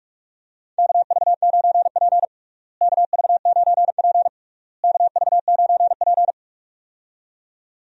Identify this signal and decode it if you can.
Morse code, heard as a keyed tone, about 41 wpm, 700 Hz. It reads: KV9P KV9P KV9P
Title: USA Call Signs - 4 Characters - Repeated 3x - 45wpm